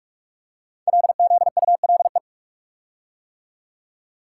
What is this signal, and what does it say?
Morse code, heard as a keyed tone, about 44 wpm, 700 Hz. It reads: LZULE
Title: Code Groups - 5 Letters - 45wpm